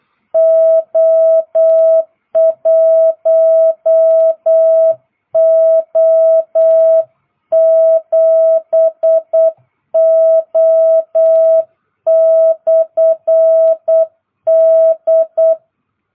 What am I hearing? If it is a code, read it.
Morse code, heard as a keyed tone, about 8 wpm, 650 Hz. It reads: O1O7O/D